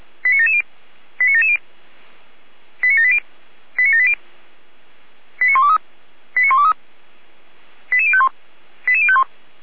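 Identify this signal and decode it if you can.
Five-tone selective call (ZVEI), heard as quick runs of five tones, tones of about 70 ms. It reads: 89770 89770 89780 89780 89123 89123 88051 88051
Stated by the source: ZVEI five-tone selective call received by SDR.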